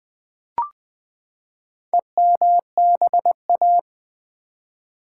Morse code, heard as a keyed tone, about 20 wpm, 700 Hz. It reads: EMBA
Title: Build A Word - 4 Letters At A Time - 12-Letter Words - 20wpm